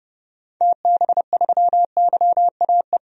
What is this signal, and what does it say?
Morse code, heard as a keyed tone, about 30 wpm, 700 Hz. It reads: TB3YAE
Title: DXCC-100 Call Signs - 6 Characters - 30wpm